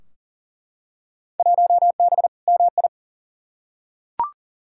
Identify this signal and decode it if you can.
Morse code, heard as a keyed tone, about 40 wpm, 700 Hz. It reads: 1B MI
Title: Field Day Exchanges - 40wpm